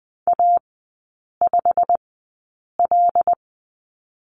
Morse code, heard as a keyed tone, about 20 wpm, spaced as for 8 wpm, 700 Hz. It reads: A5L
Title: 087 Lesson 15e - Number 5 - Three Characters 20wpm